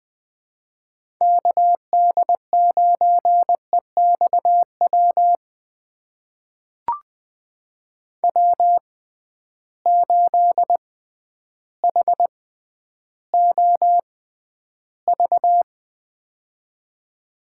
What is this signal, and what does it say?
Morse code, heard as a keyed tone, about 20 wpm, spaced as for 11 wpm, 700 Hz. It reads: KD9EXW W8HOV
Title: USA Call Signs - 6x Character Spacing - 20wpm